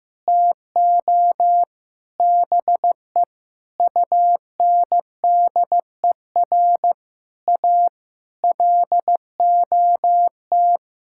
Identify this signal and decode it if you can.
Morse code, heard as a keyed tone, about 15 wpm, 700 Hz. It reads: TO BE UNDER A LOT
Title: Sentences from Top 1000 Words - Review Part 6 - 15wpm